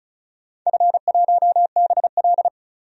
Morse code, heard as a keyed tone, about 35 wpm, 700 Hz. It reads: F1BL
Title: DXCC-100 Call Signs - 4 Characters - 35wpm